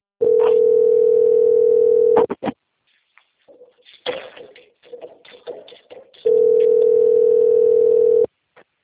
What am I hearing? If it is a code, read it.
Ringback tone (North America), 440 + 480 Hz.